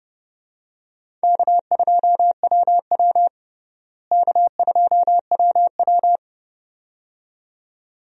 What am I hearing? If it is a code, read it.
Morse code, heard as a keyed tone, about 30 wpm, 700 Hz. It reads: K2WW K2WW
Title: USA Call Signs - 4 Characters - Repeated 2x - 30wpm